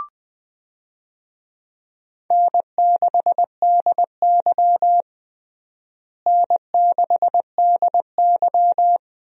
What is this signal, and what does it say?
Morse code, heard as a keyed tone, about 20 wpm, 700 Hz. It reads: N6DY N6DY
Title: USA Call Signs - 4 Characters - Repeated 3x - 20wpm